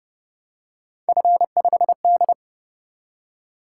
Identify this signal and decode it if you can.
Morse code, heard as a keyed tone, about 29 wpm, 700 Hz. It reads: F5D